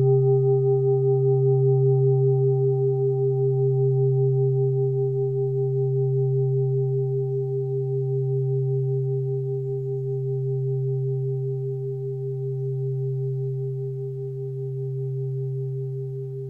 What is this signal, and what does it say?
Telephone call progress tone (Japan), heard as a telephone network's tone, about 400 Hz. Dial tone.